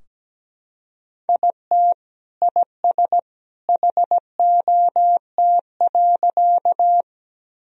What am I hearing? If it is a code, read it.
Morse code, heard as a keyed tone, about 17 wpm, 700 Hz. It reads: IT IS HOT.
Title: Sentences from Top 1000 Words - Review Part 3 - 17wpm